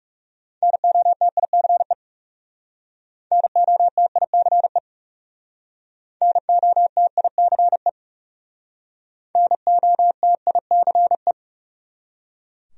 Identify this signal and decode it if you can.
Morse code, heard as a keyed tone, about 36 wpm, 700 Hz. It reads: NOTICE NOTICE NOTICE NOTICE
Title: Speed-Racing Top 500 Words 30wpm